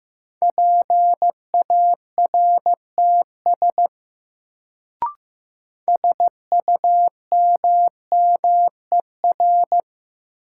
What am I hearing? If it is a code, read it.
Morse code, heard as a keyed tone, about 15 wpm, 700 Hz. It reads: PARTS SUMMER
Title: Sentences from Top 1000 Words - Review Part 6 - New Word Drill - 15wpm